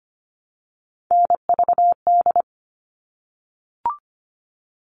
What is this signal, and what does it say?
Morse code, heard as a keyed tone, about 25 wpm, 700 Hz. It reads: NVD
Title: All Cut Numbers - 3-Digit 25wpm